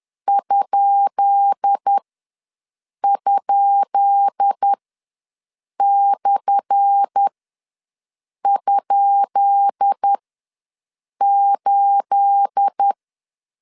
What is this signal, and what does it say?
Morse code, heard as a keyed tone, about 10 wpm, spaced as for 6 wpm, 800 Hz. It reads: ??/?8
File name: LEZIONI / 10 SIGNS / 10B INT ws 10wpm 4134